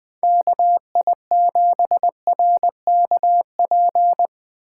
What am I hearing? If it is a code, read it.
Morse code, heard as a keyed tone, about 21 wpm, 700 Hz. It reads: KI7RKP
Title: Speed Racing - USA 2x3 Call Signs 17wpm